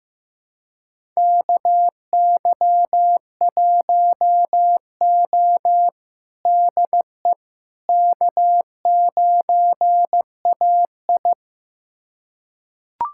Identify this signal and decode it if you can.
Morse code, heard as a keyed tone, about 15 wpm, 700 Hz. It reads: KY1O DE K9AI
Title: USA Callsign DE Callsign - 4 Characters - 15wpm